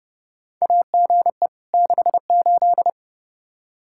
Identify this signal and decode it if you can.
Morse code, heard as a keyed tone, about 30 wpm, 700 Hz. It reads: AGE 68
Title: Practice QSO - Exchange 3 - More Info - 30wpm